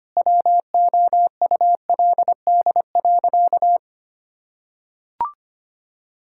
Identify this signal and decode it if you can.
Morse code, heard as a keyed tone, about 25 wpm, 700 Hz. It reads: WOULD.
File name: practice / 25wpm / Sentences with 9-Word Phrases - 25wpm